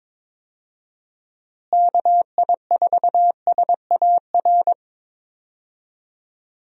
Morse code, heard as a keyed tone, about 22 wpm, 700 Hz. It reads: KI4SAR
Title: USA Call Signs - 6 Characters - Repeated 3x - 22wpm